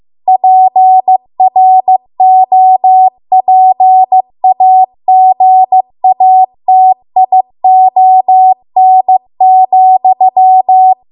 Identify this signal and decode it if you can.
Morse code, heard as a keyed tone, about 15 wpm, 750 Hz. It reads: PROPAGATION,
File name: Archive / 15 WPM / 15WPM